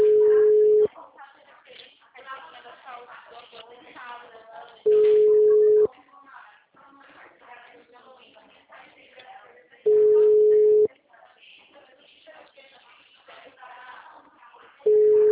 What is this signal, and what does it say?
Ringback tone (Europe, 425 Hz), with speech.
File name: out- -2010- - - .25187